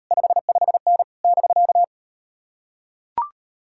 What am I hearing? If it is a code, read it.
Morse code, heard as a keyed tone, about 39 wpm, 700 Hz. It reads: 55N <BK>